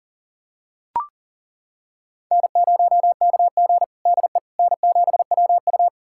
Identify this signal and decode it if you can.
Morse code, heard as a keyed tone, about 40 wpm, 700 Hz. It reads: N0KG DE N7WU
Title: USA Callsign DE Callsign - 4 Characters - 40wpm